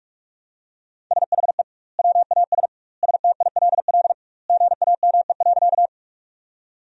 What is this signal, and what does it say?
Morse code, heard as a keyed tone, about 46 wpm, 700 Hz. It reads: SHE WAS STILL GAME.
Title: 2x Word-Spacing - Sentences from Top 500 Words - 45wpm